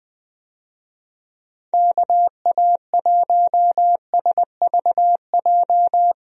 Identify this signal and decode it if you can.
Morse code, heard as a keyed tone, about 20 wpm, 700 Hz. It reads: KA1SVJ
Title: USA Call Signs - 6 Characters - Repeated 2x - 20wpm